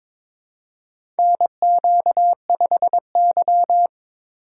Morse code, heard as a keyed tone, about 22 wpm, 700 Hz. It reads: NQ5Y
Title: USA Call Signs - 4 Characters - 2x Character Spacing - 22wpm